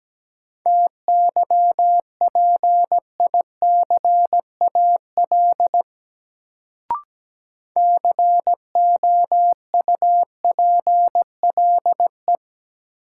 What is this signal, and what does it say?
Morse code, heard as a keyed tone, about 17 wpm, 700 Hz. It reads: TYPICAL COUPLE
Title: Sentences from Top 1100 Words - New Word Drill - 17wpm